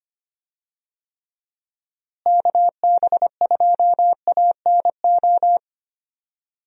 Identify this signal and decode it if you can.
Morse code, heard as a keyed tone, about 25 wpm, 700 Hz. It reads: KB2ANO
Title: USA Call Signs - 6 Characters - Repeated 3x - 25wpm